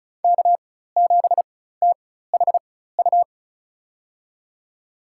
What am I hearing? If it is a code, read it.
Morse code, heard as a keyed tone, about 36 wpm, spaced as for 16 wpm, 700 Hz. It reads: K7THU
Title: USA Call Signs - 5 Characters - 4x Character Spacing - 35wpm